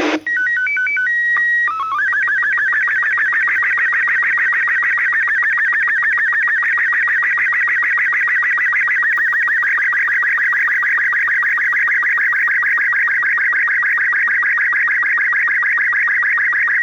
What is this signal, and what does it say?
SSTV (slow-scan television), Robot 36.
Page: ISS SSTV April 11-16 Humans in Space